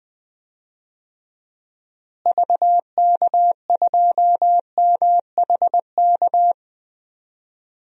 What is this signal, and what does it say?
Morse code, heard as a keyed tone, about 21 wpm, 700 Hz. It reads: VK2MHK
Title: DXCC-100 Call Signs - 6 Characters - Repeated 2x - 20wpm